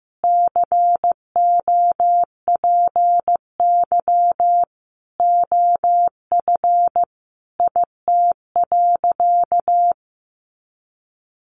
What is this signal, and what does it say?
Morse code, heard as a keyed tone, about 14 wpm, 700 Hz. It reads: COPY OF IT.
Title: Sentences from Top 1400 Words 15wpm